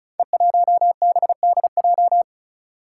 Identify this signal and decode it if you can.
Morse code, heard as a keyed tone, about 36 wpm, 700 Hz. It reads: E1BDJ
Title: Code Groups - 5 Letter:Numbers - 35wpm